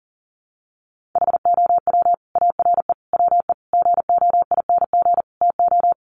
Morse code, heard as a keyed tone, about 40 wpm, 700 Hz. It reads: HOW ARE WE GOING TO
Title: Sets of 5 Words Encore 40wpm